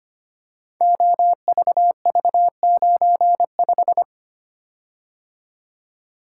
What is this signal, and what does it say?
Morse code, heard as a keyed tone, about 25 wpm, 700 Hz. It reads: OVV95
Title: Code Groups - 5 Letter:Numbers - 25wpm